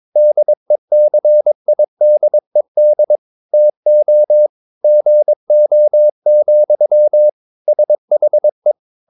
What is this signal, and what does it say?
Morse code, heard as a keyed tone, about 22 wpm, 600 Hz. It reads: DECIDED TO GO, SHE
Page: Romanen Never af Ken Follet på engelsk.